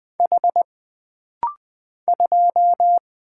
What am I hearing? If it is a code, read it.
Morse code, heard as a keyed tone, about 20 wpm, 700 Hz. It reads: H 2
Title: Single Letter-Number - Rapid-Fire 20wpm